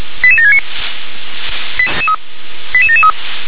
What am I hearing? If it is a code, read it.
Five-tone selective call (ZVEI), heard as quick runs of five tones, tones of about 80 ms. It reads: 88769 88773